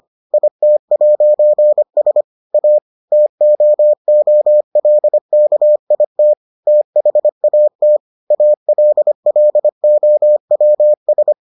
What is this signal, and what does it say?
Morse code, heard as a keyed tone, about 25 wpm, 600 Hz. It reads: IT'S A TOOLKIT THAT ALLOWS